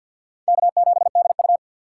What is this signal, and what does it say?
Morse code, heard as a keyed tone, about 52 wpm, 700 Hz. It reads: K7DU
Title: USA Call Signs - 4x Character Spacing - 50wpm